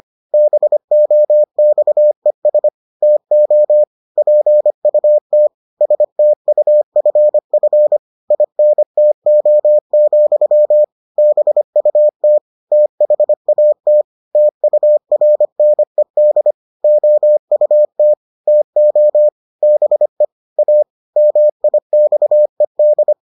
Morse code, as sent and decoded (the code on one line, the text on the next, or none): -... --- -..- . ... / - --- / .--. ..- - / ... - ..- ..-. ..-. / .. -. - --- --..-- / -... ..- - / - .... .- - / - ..- .-. -. -.. / --- ..- - / - --- / -... / .- / -- .. -..- . -..
BOXES TO PUT STUFF INTO, BUT THAT TURNED OUT TO BE A MIXED